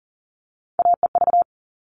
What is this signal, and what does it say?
Morse code, heard as a keyed tone, about 40 wpm, 700 Hz. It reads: AEV
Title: All Cut Numbers - 3-Digit 40wpm